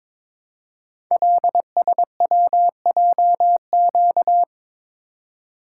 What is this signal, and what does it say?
Morse code, heard as a keyed tone, about 22 wpm, 700 Hz. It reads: LSWJQ
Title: Code Groups - 5 Letter:Numbers - 22wpm